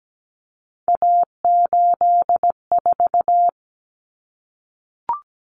Morse code, heard as a keyed tone, about 17 wpm, 700 Hz. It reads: A84
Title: Common Cut Numbers - 3-Digit 17wpm